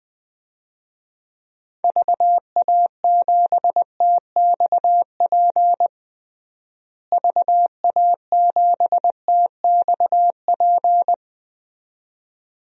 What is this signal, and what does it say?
Morse code, heard as a keyed tone, about 20 wpm, 700 Hz. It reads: VA7TXP VA7TXP
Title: DXCC-100 Call Signs - 6 Characters - Repeated 2x - 20wpm